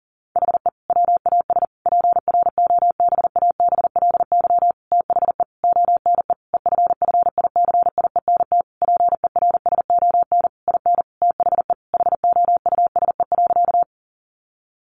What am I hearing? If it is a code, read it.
Morse code, heard as a keyed tone, about 40 wpm, 700 Hz. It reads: HE WAS PROBABLY THE ONE EFFICIENT PERSON IN THE HOUSE.
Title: Sentences from Top 1300 Words 40wpm